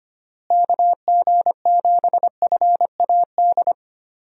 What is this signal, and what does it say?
Morse code, heard as a keyed tone, about 25 wpm, 700 Hz. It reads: KG7FAD
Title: USA Call Signs - 6 Characters - Repeated 2x - 25wpm